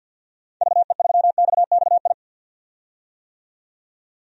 Morse code, heard as a keyed tone, about 47 wpm, 700 Hz. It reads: VE3XXI
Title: DXCC-100 Call Signs - 6 Characters - 50wpm